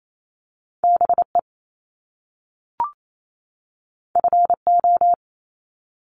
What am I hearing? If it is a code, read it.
Morse code, heard as a keyed tone, about 28 wpm, 700 Hz. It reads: BE FO
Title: Top Two Letter Combinations in Words 28wpm